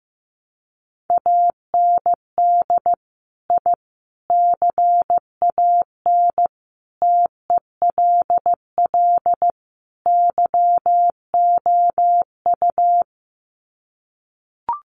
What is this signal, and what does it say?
Morse code, heard as a keyed tone, about 15 wpm, 700 Hz. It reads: AND I CAN TELL YOU
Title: Sets of 5 Words Encore2 15wpm